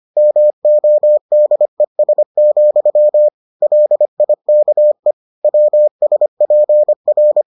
Morse code, heard as a keyed tone, about 25 wpm, 600 Hz. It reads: MODES, LIKE WSPR